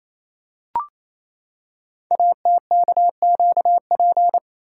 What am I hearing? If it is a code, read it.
Morse code, heard as a keyed tone, about 28 wpm, 700 Hz. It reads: ATKQP
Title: Code Groups - 5 Letters - 28wpm